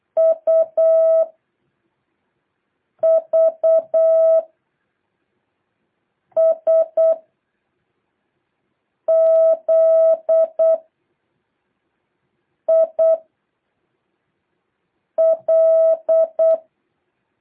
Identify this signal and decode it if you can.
Morse code, heard as a keyed tone, about 8 wpm, 650 Hz. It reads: U V S Z I L
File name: jh8cbh / 2023cw